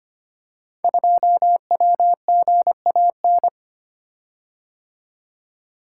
Morse code, heard as a keyed tone, about 25 wpm, 700 Hz. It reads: 2WGAN
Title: Code Groups - 5 Letter:Numbers - 25wpm